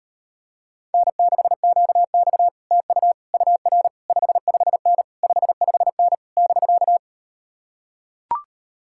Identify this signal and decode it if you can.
Morse code, heard as a keyed tone, about 38 wpm, 700 Hz. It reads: N6QX TU UR 55N 55N <BK>